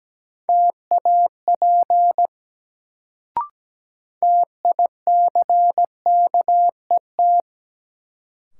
Morse code, heard as a keyed tone, about 17 wpm, 700 Hz. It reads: TAP TICKET
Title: Sentences from Top 1400 Words - New Word Drill - 17wpm